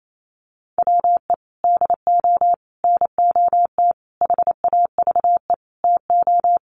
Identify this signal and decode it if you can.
Morse code, heard as a keyed tone, about 28 wpm, 700 Hz. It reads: WE DO NOT HAVE TO
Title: Sets of 5 Words 28wpm